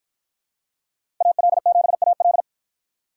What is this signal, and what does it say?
Morse code, heard as a keyed tone, about 51 wpm, 700 Hz. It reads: AL7AL